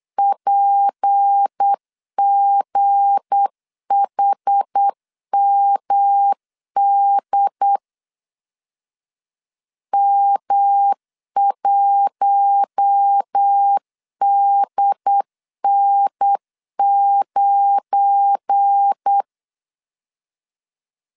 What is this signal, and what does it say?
Morse code, heard as a keyed tone, about 8 wpm, 800 Hz. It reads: PGHMD M1DN9
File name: LEZIONI / 7 GF9Z / 7 GF9 8wpm 3545